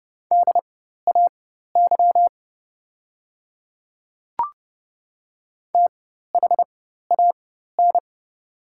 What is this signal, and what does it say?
Morse code, heard as a keyed tone, about 30 wpm, spaced as for 14 wpm, 700 Hz. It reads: DAY THAN
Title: 4x Character Spacing - Top 100 Words - 30wpm